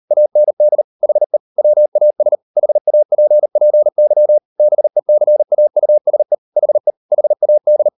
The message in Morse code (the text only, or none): AND HE WAS HAPPY BECAUSE HE HAD